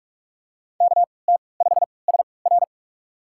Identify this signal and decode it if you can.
Morse code, heard as a keyed tone, about 46 wpm, spaced as for 26 wpm, 700 Hz. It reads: KT5SR
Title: USA Call Signs - 3x Character Spacing - 45wpm